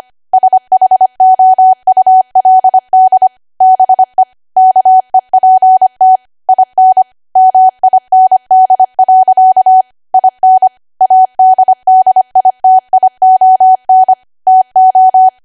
Morse code, heard as a keyed tone, about 25 wpm, 750 Hz. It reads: SHOULD BE KEPT IN MIND. IN ADDITION TO
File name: Archive / 25 WPM / 25WPM